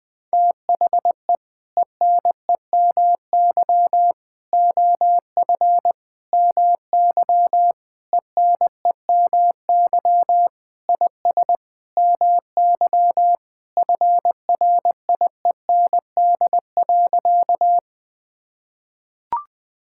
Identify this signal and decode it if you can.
Morse code, heard as a keyed tone, about 20 wpm, 700 Hz. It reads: THE ENEMY OF MY ENEMY IS MY FRIEND.
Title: Sentences with 8-Word Phrases - 20wpm